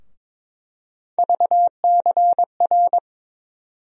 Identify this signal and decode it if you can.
Morse code, heard as a keyed tone, about 22 wpm, 700 Hz. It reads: VCR
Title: USA Call Sign Postfixes - 3 Letters 22wpm